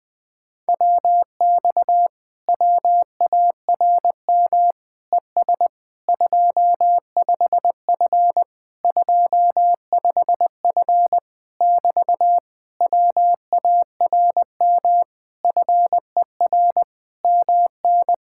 Morse code, heard as a keyed tone, about 20 wpm, 700 Hz. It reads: WX WARM ES 25F 25F <BT> WARM FER MN
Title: Speed-Racing Common QSO Phrases Encore - 20wpm